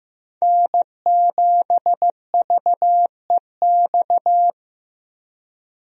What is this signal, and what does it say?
Morse code, heard as a keyed tone, about 15 wpm, 700 Hz. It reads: N7VEX
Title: USA Call Signs - 5 Characters - Repeated 2x - 15wpm